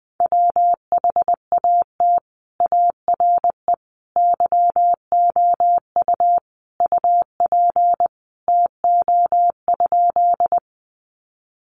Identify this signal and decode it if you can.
Morse code, heard as a keyed tone, about 20 wpm, 700 Hz. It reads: WHAT ARE YOU UP TO?
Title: Sentences from Top 100 Words 20wpm